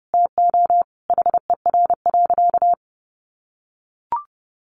Morse code, heard as a keyed tone, about 30 wpm, 700 Hz. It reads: TO HER.
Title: Sentences from Top 300 Words 30wpm